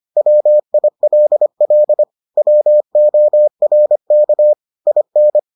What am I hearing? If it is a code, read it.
Morse code, heard as a keyed tone, about 25 wpm, 600 Hz. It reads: WILL WORK IN